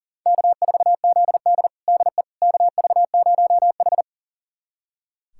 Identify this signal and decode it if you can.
Morse code, heard as a keyed tone, about 40 wpm, 700 Hz. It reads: K4ZD DE KV0H
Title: USA Callsign DE Callsign - 4 Characters - 40wpm